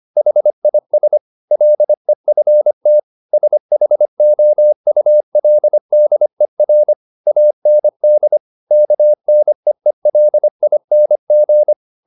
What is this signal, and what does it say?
Morse code, heard as a keyed tone, about 25 wpm, 600 Hz. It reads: HIS LEFT SHOULDER AND KNEELING
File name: the old man and the sea part 1 25wpm